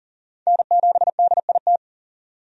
Morse code, heard as a keyed tone, about 40 wpm, 700 Hz. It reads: N7DIT
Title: USA Call Signs - 5 Characters - 40wpm